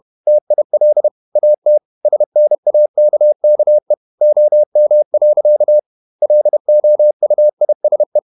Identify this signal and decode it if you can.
Morse code, heard as a keyed tone, about 31 wpm, 600 Hz. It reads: TIL AT SNAKKE OM. LOUISE
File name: Ferie QRQ 31wpm